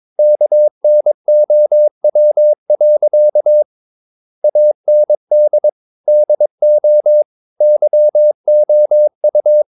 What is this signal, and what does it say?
Morse code, heard as a keyed tone, about 22 wpm, 600 Hz. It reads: KNOW. AND DO YOU
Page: Romanen Never af Ken Follet på engelsk.